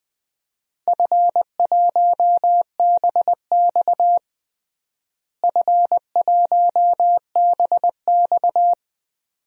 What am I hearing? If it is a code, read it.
Morse code, heard as a keyed tone, about 20 wpm, 700 Hz. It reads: F1BX F1BX
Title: DXCC-100 Call Signs - 4 Characters - Repeated 2x - 20wpm